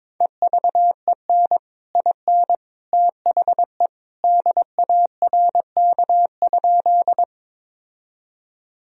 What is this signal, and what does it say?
Morse code, heard as a keyed tone, about 22 wpm, 700 Hz. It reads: EVEN IN THE DARK?
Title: Sentences from Top 1000 Words - Review Part 1 - 22wpm